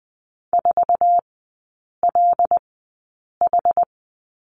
Morse code, heard as a keyed tone, about 20 wpm, spaced as for 8 wpm, 690 Hz. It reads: 4LH
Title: 081 Lesson 14e - Number 2 - Three Characters 20wpm